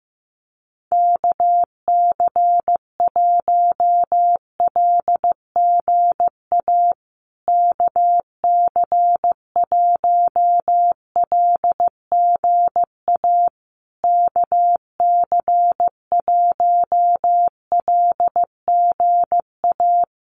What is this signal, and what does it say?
Morse code, heard as a keyed tone, about 15 wpm, 700 Hz. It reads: KC1LGA KC1LGA KC1LGA
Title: USA Call Signs Repeated 3x 15wpm